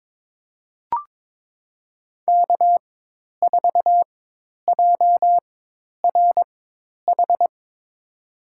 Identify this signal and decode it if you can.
Morse code, heard as a keyed tone, about 22 wpm, spaced as for 10 wpm, 700 Hz. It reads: K4JRH
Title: USA Call Signs - 5 Characters - 4x Character Spacing - 22wpm